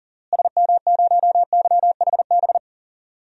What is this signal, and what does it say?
Morse code, heard as a keyed tone, about 40 wpm, 700 Hz. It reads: SM0YHB
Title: DXCC-100 Call Signs - 6 Characters - Repeated 2x - 40wpm